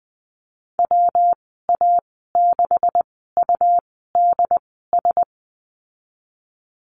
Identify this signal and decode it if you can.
Morse code, heard as a keyed tone, about 20 wpm, spaced as for 15 wpm, 700 Hz. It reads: WA6UDS
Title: 200 Lesson 32g - Letter k - Call Signs 20wpm